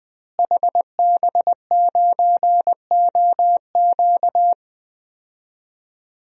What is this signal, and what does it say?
Morse code, heard as a keyed tone, about 20 wpm, 700 Hz. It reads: HB9OQ
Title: DXCC-100 Call Signs - 5 Characters - Repeated 3x - 20wpm